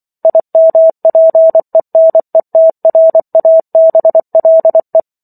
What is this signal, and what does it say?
Morse code, heard as a keyed tone, about 24 wpm, 650 Hz. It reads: IMPENETRABLE
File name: Chapter 15 - 24 WPM